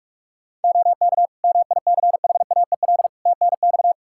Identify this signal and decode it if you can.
Morse code, heard as a keyed tone, about 41 wpm, 700 Hz. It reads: OK MICHAEL TNX
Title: K1USN SST QSO Practice 45wpm